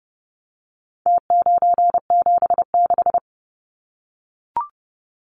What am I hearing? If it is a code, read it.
Morse code, heard as a keyed tone, about 30 wpm, 700 Hz. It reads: T976
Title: Numbers - Contest Style - 4-Digit 30wpm